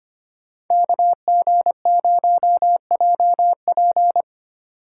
Morse code, heard as a keyed tone, about 30 wpm, 700 Hz. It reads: KG0JP
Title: Speed Racing - USA 2x2 Call Signs 25wpm